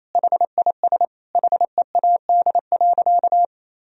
Morse code, heard as a keyed tone, about 28 wpm, 700 Hz. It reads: HIS HEAD.
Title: Sentences from Top 1000 Words - Review Part 2 - 28wpm